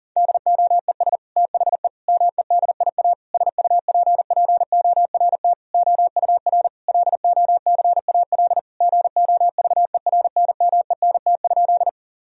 Morse code, heard as a keyed tone, about 40 wpm, 700 Hz. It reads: DOES THE MEDIA SUPPORT OUR LOCAL GOVERNMENT?
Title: Sentences from Top 1000 Words - Review Part 4 - 40wpm